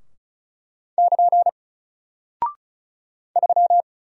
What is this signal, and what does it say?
Morse code, heard as a keyed tone, about 36 wpm, 700 Hz. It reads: <KN> 3
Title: Single Character (Letter:Number:Punctuation:Prosign) Alt 35wpm